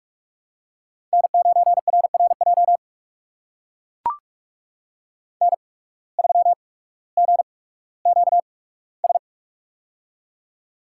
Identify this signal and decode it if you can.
Morse code, heard as a keyed tone, about 42 wpm, spaced as for 19 wpm, 700 Hz. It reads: N9RRJ N3GQS
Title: USA Call Signs - 5 Characters - 8x Character Spacing - 45wpm